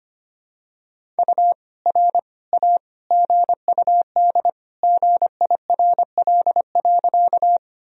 Morse code, heard as a keyed tone, about 25 wpm, 700 Hz. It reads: U R A GUD GIRL.
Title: Sentences from Top 400 Words and Abbreviations 25wpm